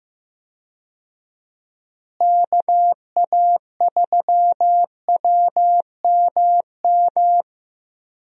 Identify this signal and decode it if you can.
Morse code, heard as a keyed tone, about 15 wpm, 700 Hz. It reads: KA3WMM
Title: USA Call Signs - 6 Characters - Repeated 2x - 15wpm